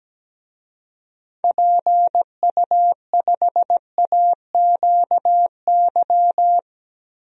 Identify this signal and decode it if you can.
Morse code, heard as a keyed tone, about 17 wpm, 700 Hz. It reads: PU5AQY
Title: DXCC-100 Call Signs - 6 Characters - Repeated 2x - 17wpm